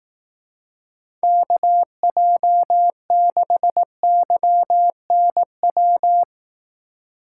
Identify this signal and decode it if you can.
Morse code, heard as a keyed tone, about 18 wpm, 700 Hz. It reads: KJ6YNW
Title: Speed Racing - USA Call Signs 15wpm